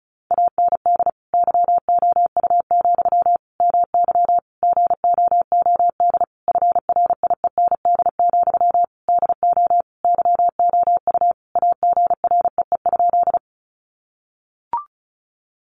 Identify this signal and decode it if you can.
Morse code, heard as a keyed tone, about 34 wpm, 700 Hz. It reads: AND YOU, MY GOOD FRIEND, DO YOU AGREE?
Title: Sentences from Top 1100 Words 35wpm